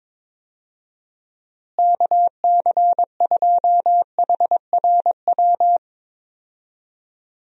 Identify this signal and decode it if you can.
Morse code, heard as a keyed tone, about 22 wpm, 700 Hz. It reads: KC2HRW
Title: USA Call Signs - 6 Characters - Repeated 3x - 22wpm